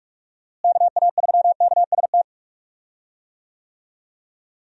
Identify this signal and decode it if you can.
Morse code, heard as a keyed tone, about 46 wpm, 700 Hz. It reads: KA3KST
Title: USA Call Signs - 6 Characters - Repeated 3x - 45wpm